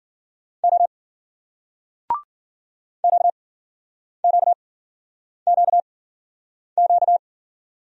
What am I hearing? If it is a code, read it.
Morse code, heard as a keyed tone, about 54 wpm, spaced as for 9 wpm, 700 Hz. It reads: O QQQQ